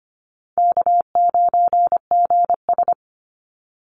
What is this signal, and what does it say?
Morse code, heard as a keyed tone, about 25 wpm, 700 Hz. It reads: K9GS
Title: Call Signs - 25wpm